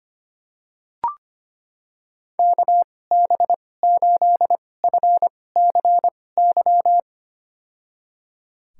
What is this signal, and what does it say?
Morse code, heard as a keyed tone, about 25 wpm, spaced as for 18 wpm, 700 Hz. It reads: KB8FCY
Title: USA Call Signs - 6 Characters - 2x Character Spacing - 25wpm